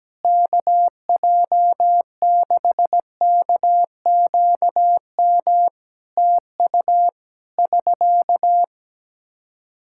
Morse code, heard as a keyed tone, about 17 wpm, 700 Hz. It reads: KJ6KQM TU <SK>
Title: Practice QSO - Exchange 4 - Ending - 17wpm